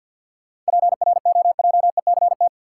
Morse code, heard as a keyed tone, about 46 wpm, 700 Hz. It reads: PROJECT
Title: Sentences from Top 1400 Words - New Word Drill - 50wpm